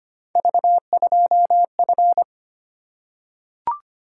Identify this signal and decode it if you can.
Morse code, heard as a keyed tone, about 25 wpm, 700 Hz. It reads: V2F
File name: Three Letter-Number Combinations - 25wpm